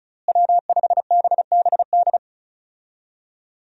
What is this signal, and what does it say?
Morse code, heard as a keyed tone, about 36 wpm, 700 Hz. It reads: W5BBD
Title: USA Call Signs - 5 Characters - Repeated 2x - 35wpm